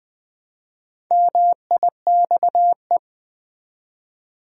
Morse code, heard as a keyed tone, about 21 wpm, 700 Hz. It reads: MIXE
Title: Build A Word - 1 Letter At A Time - 5-Letter Words - 20wpm